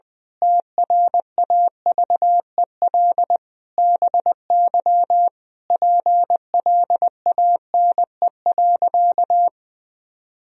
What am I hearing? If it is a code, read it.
Morse code, heard as a keyed tone, about 21 wpm, 700 Hz. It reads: TRAVEL BY PLANE.
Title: Sentences from Top 1000 Words - Review Part 3 - 20wpm